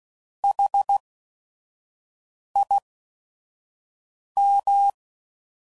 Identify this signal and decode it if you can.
Morse code, heard as a keyed tone, about 16 wpm, spaced as for 5 wpm, 800 Hz. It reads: HIM
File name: Prefixes / By First Letter / S vs Hs